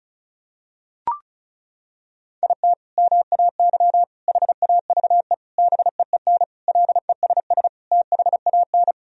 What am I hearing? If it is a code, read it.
Morse code, heard as a keyed tone, about 34 wpm, 700 Hz. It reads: IT MAY HAVE BEEN LESS THAN
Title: Sentences from Top 1000 Words - Review Part 2 - 35wpm